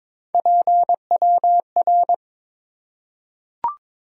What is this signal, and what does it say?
Morse code, heard as a keyed tone, about 22 wpm, 700 Hz. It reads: PWR
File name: Top 90 QSO Elements - 3x Character Spacing - 22wpm